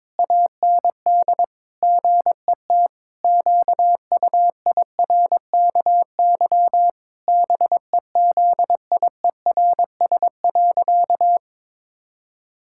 Morse code, heard as a keyed tone, about 22 wpm, 700 Hz. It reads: AND GET QUIRKY BEZIERS.
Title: Pangram Challenge v2 - 22wpm